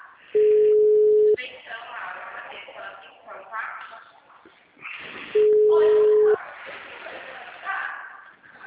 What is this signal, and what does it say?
Ringback tone (Europe, 425 Hz), with speech.
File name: out- -2017- - - .7574